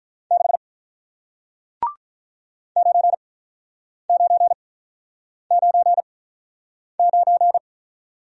Morse code, heard as a keyed tone, about 48 wpm, spaced as for 9 wpm, 700 Hz. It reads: 6 9999